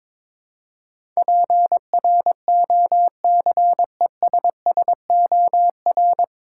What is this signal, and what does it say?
Morse code, heard as a keyed tone, about 22 wpm, 700 Hz. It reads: PROCESSOR
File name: 9 Letter Words - First 3 Letters Given - 22wpm